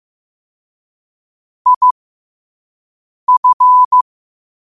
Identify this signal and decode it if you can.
Morse code, heard as a keyed tone, about 15 wpm, spaced as for 5 wpm, 1000 Hz. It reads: IF
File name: qotdfiles / 5wpm